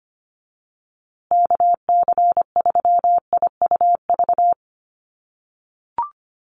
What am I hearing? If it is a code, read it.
Morse code, heard as a keyed tone, about 25 wpm, 700 Hz. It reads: KC3IUV
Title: USA Call Signs Repeated 2x 25wpm